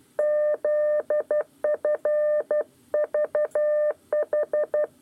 Morse code, heard as a keyed tone, about 12 wpm, 600 Hz. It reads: ZFVH
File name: 8 lezione cw 1